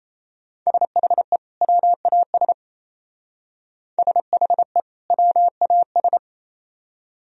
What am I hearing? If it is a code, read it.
Morse code, heard as a keyed tone, about 34 wpm, 700 Hz. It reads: SHE WAS SHE WAS
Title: Speed-Racing - Sets of 2 Words 28wpm